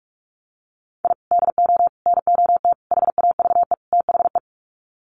Morse code, heard as a keyed tone, about 44 wpm, 700 Hz. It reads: I DO NOT HAVE THE
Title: Sets of 5 Words 45wpm